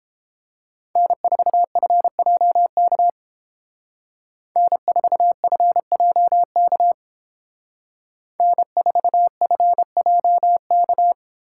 Morse code, heard as a keyed tone, about 29 wpm, 700 Hz. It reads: N4FJK N4FJK N4FJK